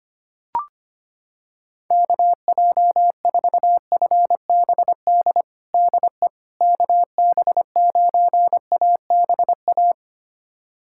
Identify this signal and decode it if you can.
Morse code, heard as a keyed tone, about 25 wpm, 700 Hz. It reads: KJ4FBD DE KB9ABA
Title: USA Callsign DE Callsign - 6 Characters - 25wpm